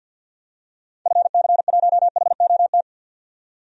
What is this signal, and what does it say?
Morse code, heard as a keyed tone, about 48 wpm, 700 Hz. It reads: FC1HOT
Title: DXCC-100 Call Signs - 6 Characters - 50wpm